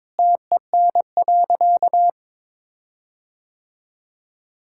Morse code, heard as a keyed tone, about 22 wpm, 700 Hz. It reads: TEN.
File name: Sentences with 7-Word Phrases - 22wpm